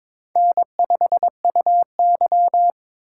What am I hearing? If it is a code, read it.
Morse code, heard as a keyed tone, about 21 wpm, 700 Hz. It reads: N5UY
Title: USA Call Signs - 4x Character Spacing - 22wpm